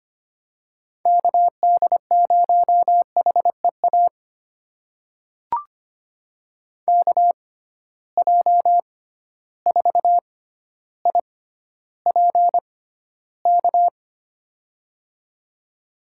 Morse code, heard as a keyed tone, about 25 wpm, spaced as for 13 wpm, 700 Hz. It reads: KD0HEA KJ4IPK
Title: USA Call Signs - 6 Characters - 6x Character Spacing - 25wpm